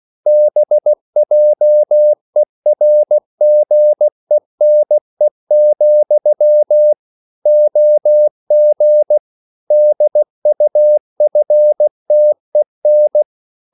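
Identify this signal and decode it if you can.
Morse code, heard as a keyed tone, about 16 wpm, 600 Hz. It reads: BJERGENE, OG DUFTEN
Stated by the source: Slow (QRS) Morse code at 16 wpm, from CW med Gnister.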